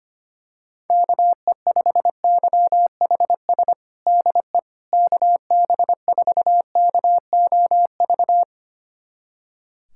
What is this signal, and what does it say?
Morse code, heard as a keyed tone, about 25 wpm, 700 Hz. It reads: KE5YHS DE KB4KOV
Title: USA Callsign DE Callsign - 6 Characters - 25wpm